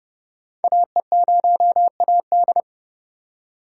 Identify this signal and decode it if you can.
Morse code, heard as a keyed tone, about 30 wpm, 700 Hz. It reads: AE0AD
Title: Code Groups - 5 Letter:Numbers - 30wpm